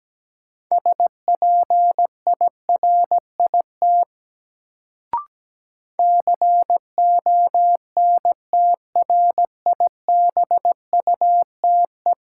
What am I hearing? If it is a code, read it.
Morse code, heard as a keyed tone, about 17 wpm, 700 Hz. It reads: SPIRIT CONTRIBUTE
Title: Sentences from Top 1400 Words - New Word Drill Alt - 17wpm